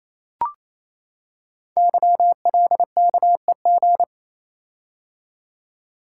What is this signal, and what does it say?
Morse code, heard as a keyed tone, about 27 wpm, 700 Hz. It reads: YLKEG
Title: Code Groups - 5 Letters - 28wpm